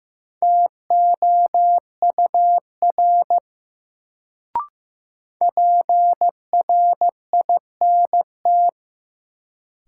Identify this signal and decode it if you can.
Morse code, heard as a keyed tone, about 16 wpm, 700 Hz. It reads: TOUR PRINT
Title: Sentences from Top 1400 Words - New Word Drill - 15wpm